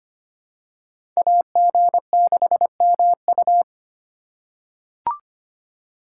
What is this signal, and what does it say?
Morse code, heard as a keyed tone, about 25 wpm, 700 Hz. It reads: AG6MU
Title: USA Call Signs - 5 Characters - 4x Character Spacing - 25wpm